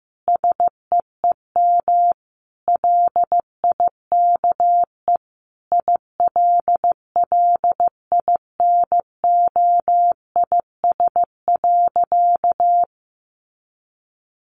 Morse code, heard as a keyed tone, about 15 wpm, 700 Hz. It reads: SEEM LIKE ILLINOIS.
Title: A Sound Of Thunder 15wpm